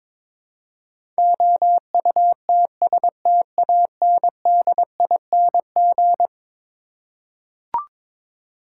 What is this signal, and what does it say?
Morse code, heard as a keyed tone, about 22 wpm, 700 Hz. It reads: OUTSTANDING
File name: practice / 22wpm / Eleven Letter Words - 22wpm